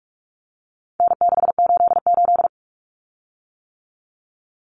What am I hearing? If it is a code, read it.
Morse code, heard as a keyed tone, about 44 wpm, 700 Hz. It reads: N688
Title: Common Cut Numbers - 4-Digit 45wpm